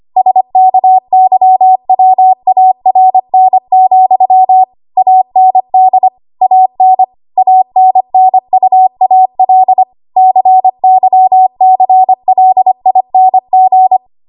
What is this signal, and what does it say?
Morse code, heard as a keyed tone, about 25 wpm, 750 Hz. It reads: SKYWARN, AND AN ANNUAL CYCLING
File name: Morse / 25WPM